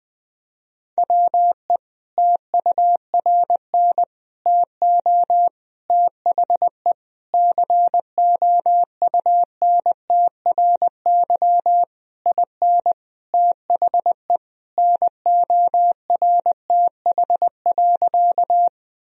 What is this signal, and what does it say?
Morse code, heard as a keyed tone, about 21 wpm, 700 Hz. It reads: WE TURN TO THE COUNTRY IN THE NORTH.
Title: Sentences from Top 1000 Words - Review Part 1 - 20wpm